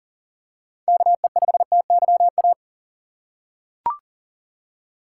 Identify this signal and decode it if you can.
Morse code, heard as a keyed tone, about 40 wpm, 700 Hz. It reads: KE5TYA
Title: USA Call Signs - 6 Characters - 3x Character Spacing - 40wpm